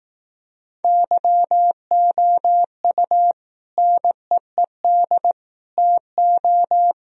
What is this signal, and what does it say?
Morse code, heard as a keyed tone, about 18 wpm, 700 Hz. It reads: YOU NEED TO
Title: Speed-Racing - Sets of 3 Words 15wpm